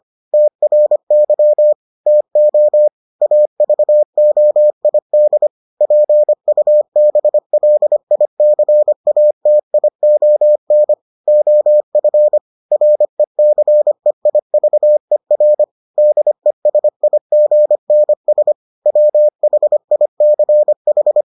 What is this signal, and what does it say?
Morse code, heard as a keyed tone, about 25 wpm, 600 Hz. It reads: TRY TO AVOID PUBLICATION OF RECEIVER DESIGNS WHICH